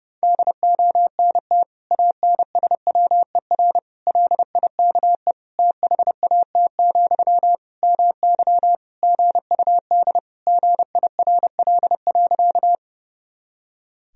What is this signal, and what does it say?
Morse code, heard as a keyed tone, about 30 wpm, 700 Hz. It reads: DONT ANSWER LIKE THAT, MY GUD GIRL.
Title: Sentences from Top 400 Words and Abbreviations 30wpm